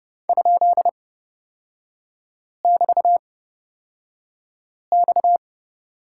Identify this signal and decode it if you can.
Morse code, heard as a keyed tone, about 30 wpm, spaced as for 5 wpm, 700 Hz. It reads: ?<BT>X
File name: Lesson 37e - Letter x - Three Characters 30wpm